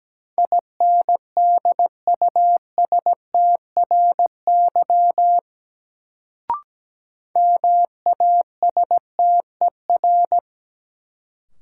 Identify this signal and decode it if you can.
Morse code, heard as a keyed tone, about 17 wpm, 700 Hz. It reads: INDUSTRY MASTER
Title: Sentences from Top 1300 Words - New Word Drill Alt - 17wpm